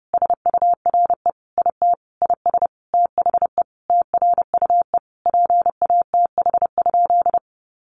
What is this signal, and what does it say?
Morse code, heard as a keyed tone, about 30 wpm, 700 Hz. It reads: SURE IT IS THE TRUE PATH?
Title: Sentences from Top 1400 Words 30wpm